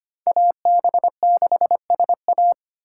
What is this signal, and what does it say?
Morse code, heard as a keyed tone, about 25 wpm, 700 Hz. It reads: AB6SA
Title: USA Call Signs - 5 Characters - 2x Character Spacing - 25wpm